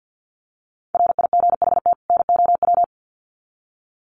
Morse code, heard as a keyed tone, about 50 wpm, 700 Hz. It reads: RIGHT NOW
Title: Sets of 2 Words 50wpm